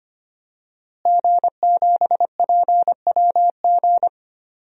Morse code, heard as a keyed tone, about 25 wpm, 700 Hz. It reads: G7PWG
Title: DXCC-100 Call Signs - 5 Characters - Repeated 3x - 25wpm